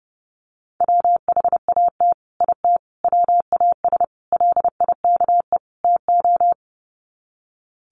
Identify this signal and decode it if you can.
Morse code, heard as a keyed tone, about 28 wpm, 700 Hz. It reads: WHAT IT WAS LIKE TO
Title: Sets of 5 Words Encore 30wpm